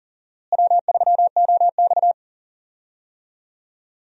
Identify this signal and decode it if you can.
Morse code, heard as a keyed tone, about 37 wpm, 700 Hz. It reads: W3OX
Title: USA Call Signs - 4 Characters - Repeated 2x - 40wpm